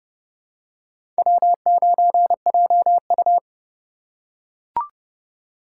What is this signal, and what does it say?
Morse code, heard as a keyed tone, about 31 wpm, 700 Hz. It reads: W9JU
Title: USA Call Signs - 2x Character Spacing - 30wpm